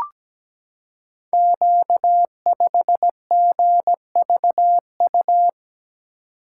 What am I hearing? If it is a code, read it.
Morse code, heard as a keyed tone, about 17 wpm, 700 Hz. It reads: Q5GVU
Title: Code Groups - 5 Letter:Numbers - 17wpm